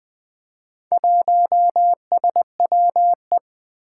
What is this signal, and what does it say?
Morse code, heard as a keyed tone, about 20 wpm, 700 Hz. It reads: 1SWE
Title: Four Challenging Letter-Number Combinations 20wpm